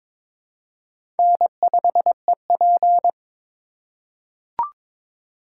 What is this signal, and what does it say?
Morse code, heard as a keyed tone, about 22 wpm, 700 Hz. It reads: N5EP
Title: USA Call Signs - 4 Characters - 2x Character Spacing - 22wpm